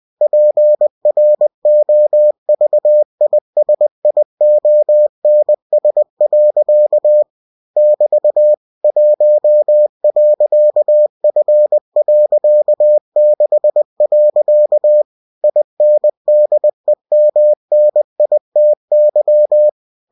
Morse code, heard as a keyed tone, about 20 wpm, 600 Hz. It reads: PROVISIONS. <BT> 1.F.6. INDEMNITY
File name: cw-books / winnie-the-pooh / winnie-20wpm-page-0119